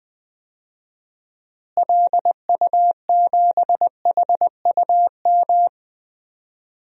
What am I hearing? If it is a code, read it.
Morse code, heard as a keyed tone, about 20 wpm, 700 Hz. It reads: LU7HUM
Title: DXCC-100 Call Signs - 6 Characters - Repeated 2x - 20wpm